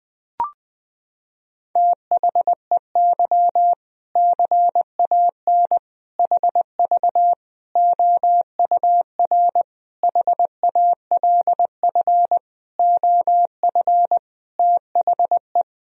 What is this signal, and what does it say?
Morse code, heard as a keyed tone, about 20 wpm, 700 Hz. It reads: THEY CAN HV OUR HALF OF THE
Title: Sentences from Top 400 Words and Abbreviations 20wpm